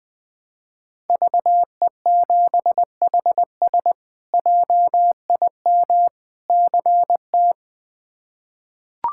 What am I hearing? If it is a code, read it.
Morse code, heard as a keyed tone, about 20 wpm, 700 Hz. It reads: VE7HS JIM CT
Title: K1USN SST QSO Practice - 20wpm